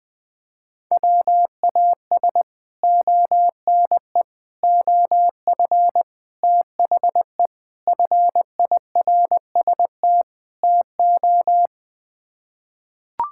Morse code, heard as a keyed tone, about 21 wpm, 700 Hz. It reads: WAS ONE OF THE FIRST TO
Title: Sets of 6 Words - 20wpm